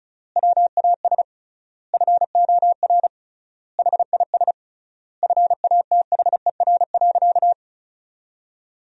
Morse code, heard as a keyed tone, about 36 wpm, 700 Hz. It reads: WAS FOR HIS FATHER.
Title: 3x Word-Spacing - Sentences from Top 500 Words - 35wpm